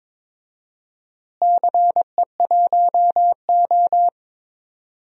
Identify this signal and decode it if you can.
Morse code, heard as a keyed tone, about 22 wpm, 700 Hz. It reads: CE1O
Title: DXCC-100 Call Signs - 4 Characters - Repeated 3x - 22wpm